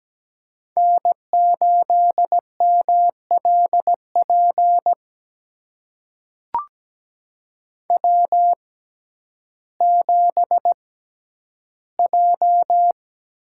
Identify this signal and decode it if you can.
Morse code, heard as a keyed tone, about 17 wpm, spaced as for 10 wpm, 700 Hz. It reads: N8MLP W7J
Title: USA Call Signs - 5 Characters - 6x Character Spacing - 17wpm